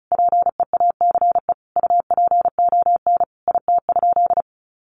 Morse code, heard as a keyed tone, about 34 wpm, 700 Hz. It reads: PEACE UPON IT?
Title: Sentences from Top 1500 Words 35wpm